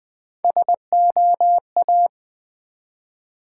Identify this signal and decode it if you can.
Morse code, heard as a keyed tone, about 20 wpm, 700 Hz. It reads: SOA
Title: Build A Word - 1 Letter At A Time - 4-Letter Words - 20wpm